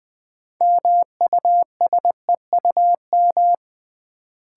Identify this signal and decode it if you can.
Morse code, heard as a keyed tone, about 20 wpm, 700 Hz. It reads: MUSEUM
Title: Build A Word - 2 Letters At A Time - 6-Letter Words - 20wpm